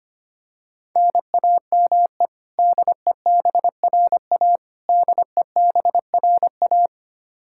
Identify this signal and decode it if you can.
Morse code, heard as a keyed tone, about 25 wpm, 700 Hz. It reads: NAME DEBRA DEBRA
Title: Practice QSO - Exchange 1 - Beginning - 25wpm